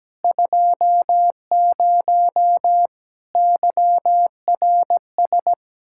Morse code, heard as a keyed tone, about 17 wpm, 700 Hz. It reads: 20 YRS
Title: K1USN SST QSO Practice 17wpm